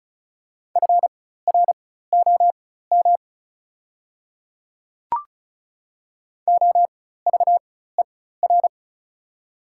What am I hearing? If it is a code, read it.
Morse code, heard as a keyed tone, about 36 wpm, spaced as for 16 wpm, 700 Hz. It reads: FROM OVER
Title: 4x Character Spacing - Top 100 Words - 35wpm